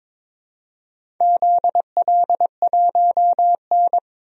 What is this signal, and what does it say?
Morse code, heard as a keyed tone, about 22 wpm, 700 Hz. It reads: ZL1N
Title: DXCC-100 Call Signs - 4 Characters - Repeated 2x - 22wpm